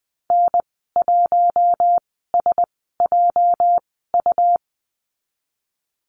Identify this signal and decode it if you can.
Morse code, heard as a keyed tone, about 20 wpm, spaced as for 15 wpm, 700 Hz. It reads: N1SJU
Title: 213 Lesson 34g - Number 8 - Call Signs 20wpm